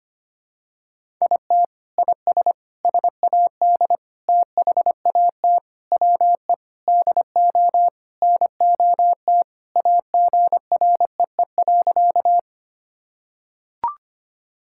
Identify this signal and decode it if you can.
Morse code, heard as a keyed tone, about 25 wpm, 700 Hz. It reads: IT IS SAD THAT WE DO NOT AGREE.
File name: practice / 25wpm / Sentences with 4-Word Phrases - 25wpm